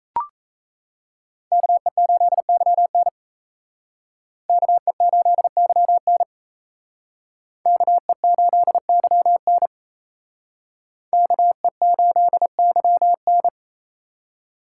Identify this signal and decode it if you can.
Morse code, heard as a keyed tone, about 35 wpm, 700 Hz. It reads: KE8YN KE8YN KE8YN KE8YN
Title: Speed Racing - USA 2x2 Call Signs 28wpm